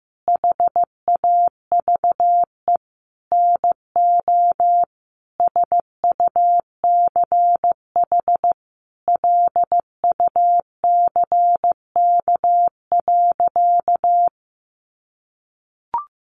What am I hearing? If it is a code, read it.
Morse code, heard as a keyed tone, about 15 wpm, 700 Hz. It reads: HAVE NO SUCH LUCK.
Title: Sentences from Top 1500 Words 15wpm